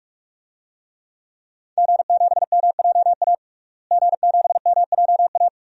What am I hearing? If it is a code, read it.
Morse code, heard as a keyed tone, about 41 wpm, 700 Hz. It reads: G7MJA G7MJA
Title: DXCC-100 Call Signs - 5 Characters - Repeated 2x - 45wpm